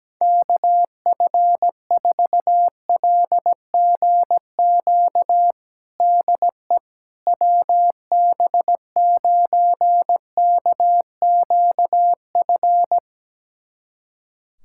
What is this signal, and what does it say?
Morse code, heard as a keyed tone, about 17 wpm, 700 Hz. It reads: KF4LGQ DE WB9KQF
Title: USA Callsign DE Callsign - 6 Characters - 17wpm